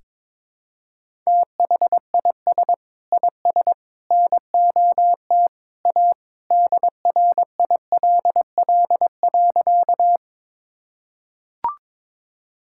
Morse code, heard as a keyed tone, about 22 wpm, 700 Hz. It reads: THIS IS NOT A DRILL.
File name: practice / 22wpm / Sentences with 3-Word Phrases - 22wpm